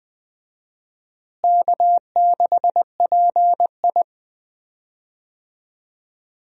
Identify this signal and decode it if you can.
Morse code, heard as a keyed tone, about 20 wpm, 700 Hz. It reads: K6PI
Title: USA Call Signs - 4 Characters - Repeated 2x - 20wpm